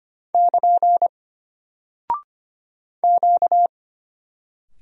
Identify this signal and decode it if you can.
Morse code, heard as a keyed tone, about 25 wpm, 700 Hz. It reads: <KN> Q
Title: Single Character (Letter:Number:Punctuation:Prosign) Alt 25wpm